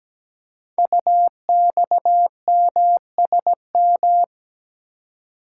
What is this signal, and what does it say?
Morse code, heard as a keyed tone, about 17 wpm, 700 Hz. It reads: UXMSM
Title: Code Groups - 5 Letters - 17wpm